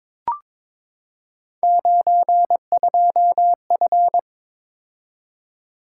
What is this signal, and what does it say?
Morse code, heard as a keyed tone, about 23 wpm, 700 Hz. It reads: 92F
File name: practice / 22wpm / Three Letter-Number Combinations - 22wpm